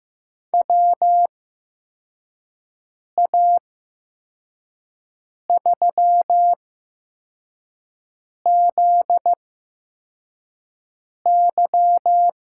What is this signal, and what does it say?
Morse code, heard as a keyed tone, about 15 wpm, spaced as for 4 wpm, 700 Hz. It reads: WA3ZY
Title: USA Call Signs - 6 Characters - 8x Character Spacing - 15wpm